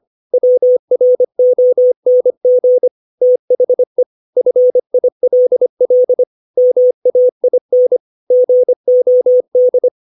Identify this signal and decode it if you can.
Morse code, heard as a keyed tone, about 25 wpm, 500 Hz. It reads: WRONG THE FILL MAIN GOD
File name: 632 words / 632 Words 25WPM